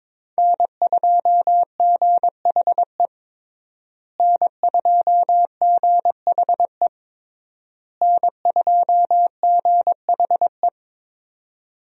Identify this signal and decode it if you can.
Morse code, heard as a keyed tone, about 22 wpm, 700 Hz. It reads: N2GHE N2GHE N2GHE
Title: USA Call Signs - 5 Characters - Repeated 3x - 22wpm